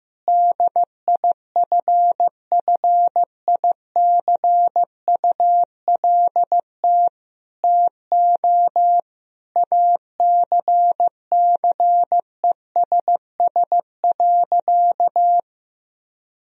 Morse code, heard as a keyed tone, about 15 wpm, 700 Hz. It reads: DIFFICULT TO ACCESS.
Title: Sentences from Top 1000 Words - Review Part 4 - 15wpm